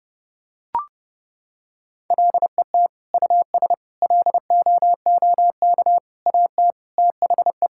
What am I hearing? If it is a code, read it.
Morse code, heard as a keyed tone, about 30 wpm, 700 Hz. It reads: LET US LOOK AT THE
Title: Sentences from Top 1000 Words - Review Part 5 - 30wpm